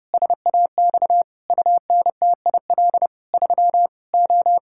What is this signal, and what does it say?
Morse code, heard as a keyed tone, about 28 wpm, 700 Hz. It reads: SAX UNTIL 3 O
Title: Pangram Challenge v2 - 30wpm